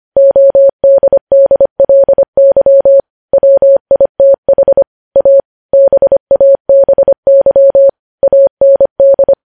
Morse code, heard as a keyed tone, about 25 wpm, 550 Hz. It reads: ODDLY WITH A BABY AND
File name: podcast / News headlines-1x-25x25